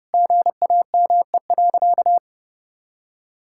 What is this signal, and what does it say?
Morse code, heard as a keyed tone, about 30 wpm, 700 Hz. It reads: GAME.
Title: Sentences from Top 1000 Words - Review Part 5 - 30wpm